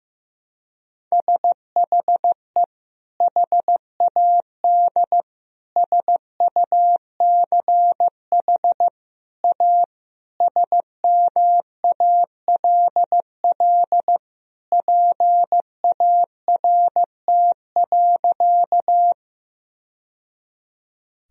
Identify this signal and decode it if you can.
Morse code, heard as a keyed tone, about 15 wpm, 700 Hz. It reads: SHE HAD SUCH A SMALL PART.
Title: Sentences from Top 1000 Words - Review Part 1 - 15wpm